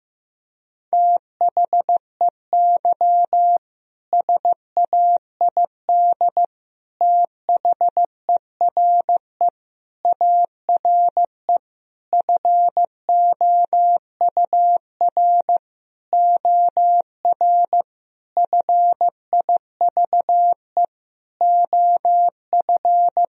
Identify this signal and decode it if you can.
Morse code, heard as a keyed tone, about 15 wpm, 700 Hz. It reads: THEY SAID THERE ARE FOUR OR FIVE OF
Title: Sentences from Top 1000 Words - Review Part 2 - 15wpm